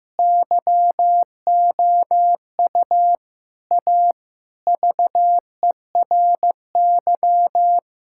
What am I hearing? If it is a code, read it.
Morse code, heard as a keyed tone, about 15 wpm, 700 Hz. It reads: YOU A VERY
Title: Sentences from Top 1000 Words - Review Part 6 - 15wpm